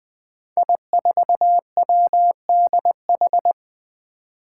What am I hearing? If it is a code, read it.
Morse code, heard as a keyed tone, about 20 wpm, 700 Hz. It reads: I4WDH
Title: Five Challenging Letter-Number Combinations 20wpm